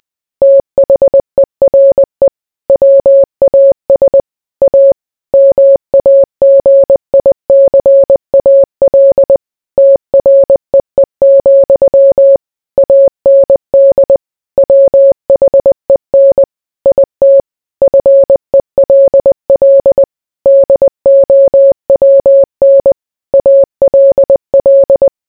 Morse code, heard as a keyed tone, about 20 wpm, 550 Hz. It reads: THERE WAS A MAGICAL TREE, AND WHEN IT FELL DOWN ALL